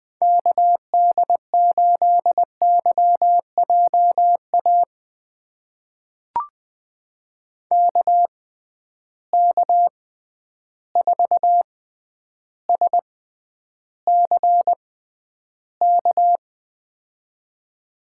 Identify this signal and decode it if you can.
Morse code, heard as a keyed tone, about 20 wpm, spaced as for 10 wpm, 700 Hz. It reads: KD8YJA KK4SCK
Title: USA Call Signs - 6 Characters - 6x Character Spacing - 20wpm